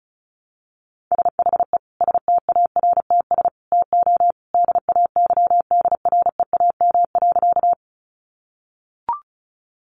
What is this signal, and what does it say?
Morse code, heard as a keyed tone, about 34 wpm, 700 Hz. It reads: SHE STARTS TO DAYDREAM.
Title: Easy Phrases 35wpm